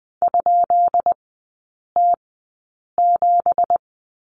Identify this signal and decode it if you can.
Morse code, heard as a keyed tone, about 20 wpm, spaced as for 8 wpm, 700 Hz. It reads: ?T7
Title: 172 Lesson 28e - Number 9 - Three Characters 20wpm